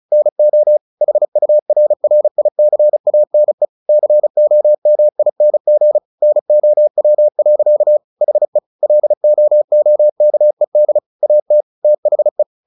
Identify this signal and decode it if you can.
Morse code, heard as a keyed tone, about 34 wpm, 600 Hz. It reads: NO HURRICANE COMING NOW. HE LOOKED AT THE
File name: the old man and the sea / the old man and the sea 1 35wpm